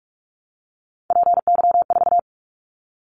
Morse code, heard as a keyed tone, about 44 wpm, 700 Hz. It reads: PY4
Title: Callsign Prefixes for Top 100 DXCC - 45wpm